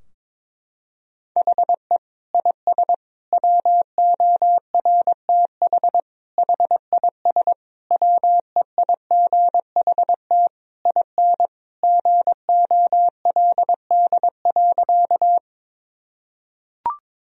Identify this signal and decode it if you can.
Morse code, heard as a keyed tone, about 22 wpm, 700 Hz. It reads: HE IS WORTH HIS WEIGHT IN GOLD.
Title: Sentences from Top 1000 Words - Review Part 3 - 22wpm